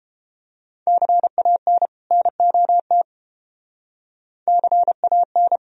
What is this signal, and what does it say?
Morse code, heard as a keyed tone, about 29 wpm, 700 Hz. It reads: CAN NOT CAN